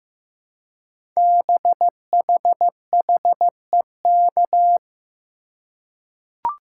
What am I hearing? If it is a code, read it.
Morse code, heard as a keyed tone, about 15 wpm, 700 Hz. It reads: BHHEK
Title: Code Groups - 5 Letters - 15wpm